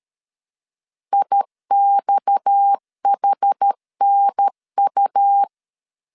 Morse code, heard as a keyed tone, about 13 wpm, 800 Hz. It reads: IXHNU
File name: CW / DET / 1 9 1500C S A 12wpm 3800